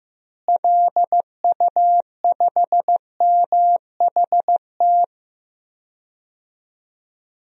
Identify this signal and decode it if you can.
Morse code, heard as a keyed tone, about 15 wpm, 700 Hz. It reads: LU5MHT
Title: DXCC-100 Call Signs - 6 Characters - Repeated 2x - 15wpm